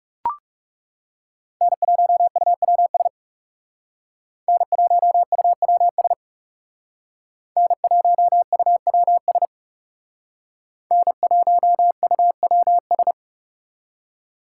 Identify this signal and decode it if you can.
Morse code, heard as a keyed tone, about 35 wpm, 700 Hz. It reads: N1UWS N1UWS N1UWS N1UWS
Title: Speed Racing - USA 1x3 Call Signs 30wpm